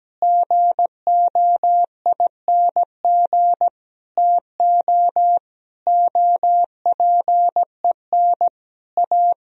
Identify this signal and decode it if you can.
Morse code, heard as a keyed tone, about 17 wpm, 700 Hz. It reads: GOING TO OPEN A
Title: Sentences from Top 1000 Words - Review Part 3 - 17wpm